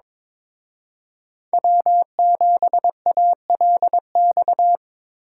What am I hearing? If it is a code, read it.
Morse code, heard as a keyed tone, about 25 wpm, 700 Hz. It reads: W7ALX
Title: Speed Racing - USA Call Signs 22wpm